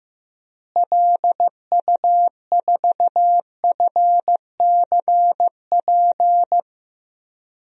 Morse code, heard as a keyed tone, about 16 wpm, 700 Hz. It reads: LU4FCP
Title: DXCC-100 Call Signs - 6 Characters - Repeated 2x - 15wpm